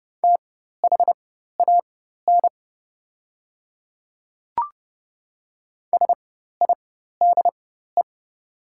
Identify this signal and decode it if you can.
Morse code, heard as a keyed tone, about 29 wpm, spaced as for 14 wpm, 700 Hz. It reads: THAN SIDE
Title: 4x Character Spacing - Top 100 Words - 30wpm